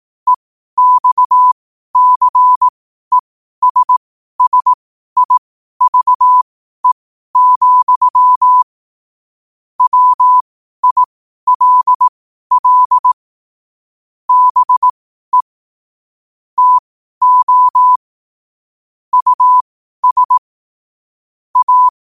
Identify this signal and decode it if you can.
Morse code, heard as a keyed tone, about 18 wpm, spaced as for 12 wpm, 1000 Hz. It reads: EXCESSIVE, WILL BE TO US A
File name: qotdfiles / 12wpm